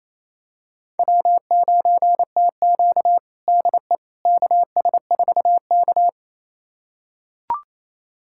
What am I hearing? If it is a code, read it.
Morse code, heard as a keyed tone, about 28 wpm, 700 Hz. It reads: W9TQ DE KS4K
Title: USA Callsign DE Callsign - 4 Characters - 28wpm